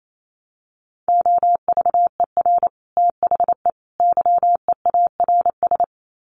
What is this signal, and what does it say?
Morse code, heard as a keyed tone, about 28 wpm, 700 Hz. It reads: OVER THE YEARS
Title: Sets of 3 Words 28wpm